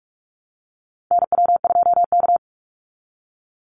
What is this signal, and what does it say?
Morse code, heard as a keyed tone, about 44 wpm, 700 Hz. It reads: NW2K
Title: Call Signs Repeated 2x - 45wpm